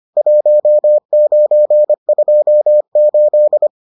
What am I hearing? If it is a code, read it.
Morse code, heard as a keyed tone, about 25 wpm, 600 Hz. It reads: 1928